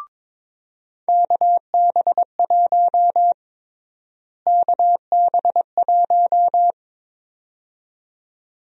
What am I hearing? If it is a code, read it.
Morse code, heard as a keyed tone, about 22 wpm, 700 Hz. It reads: KB1 KB1
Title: USA Call Sign Prefixes - Repeated 2x - 22wpm